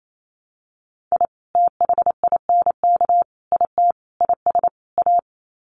Morse code, heard as a keyed tone, about 28 wpm, 700 Hz. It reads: I THINK IT IS A
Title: Sets of 5 Words Encore 28wpm